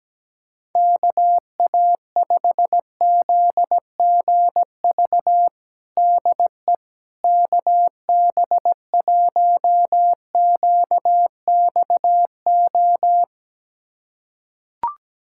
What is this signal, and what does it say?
Morse code, heard as a keyed tone, about 17 wpm, 700 Hz. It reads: KA5ZGV DE KB1QXO
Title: USA Callsign DE Callsign - 6 Characters - 17wpm